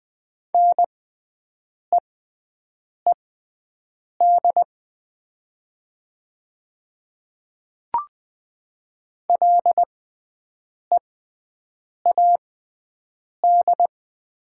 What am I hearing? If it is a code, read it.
Morse code, heard as a keyed tone, about 20 wpm, spaced as for 7 wpm, 700 Hz. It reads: NEED LEAD
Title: 6x Character Spacing - Top 500 Words - 20wpm